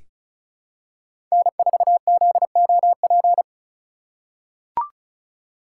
Morse code, heard as a keyed tone, about 36 wpm, 700 Hz. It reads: N4ZOP
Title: USA Call Signs - 5 Characters - 3x Character Spacing - 35wpm